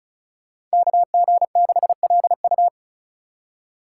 Morse code, heard as a keyed tone, about 36 wpm, 700 Hz. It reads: KG6LU
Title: USA Call Signs - 5 Characters - 4x Character Spacing - 35wpm